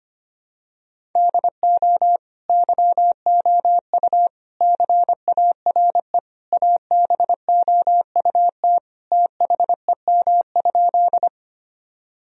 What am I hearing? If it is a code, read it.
Morse code, heard as a keyed tone, about 25 wpm, 700 Hz. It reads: DO YOU CARE ABOUT THEM?
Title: Sentences from Top 1000 Words - Review Part 1 - 25wpm